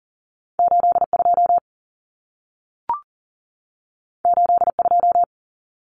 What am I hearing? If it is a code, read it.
Morse code, heard as a keyed tone, about 40 wpm, 700 Hz. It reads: 82 82
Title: Common Cut Numbers - 2-Digit 40wpm